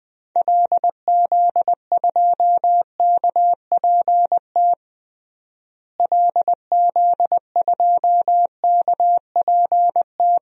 Morse code, heard as a keyed tone, about 20 wpm, 700 Hz. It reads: LZ2KPT LZ2KPT
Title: DXCC-100 Call Signs - 6 Characters - Repeated 3x - 20wpm